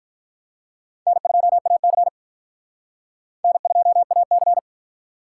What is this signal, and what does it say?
Morse code, heard as a keyed tone, about 42 wpm, 700 Hz. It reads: N2AC N2AC
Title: Speed Racing - USA 1x2 Call Signs 35wpm